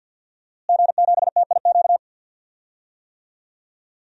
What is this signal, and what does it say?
Morse code, heard as a keyed tone, about 52 wpm, 700 Hz. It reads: G7TIQ
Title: DXCC-100 Call Signs - 5 Characters - Repeated 3x - 50wpm